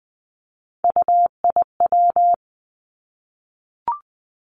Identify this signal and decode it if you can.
Morse code, heard as a keyed tone, about 20 wpm, 700 Hz. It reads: UIW
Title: Three Challenging Letter-Number Combinations 20wpm